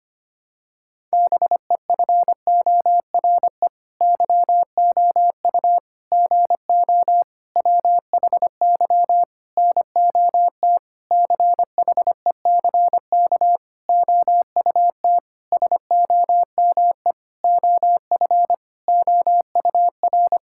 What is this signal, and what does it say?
Morse code, heard as a keyed tone, about 25 wpm, 700 Hz. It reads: BEFORE YOU GO WHY NOT CHECK OUT SOME OF OUR
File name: practice / 25wpm / Sentences with 9-Word Phrases - 25wpm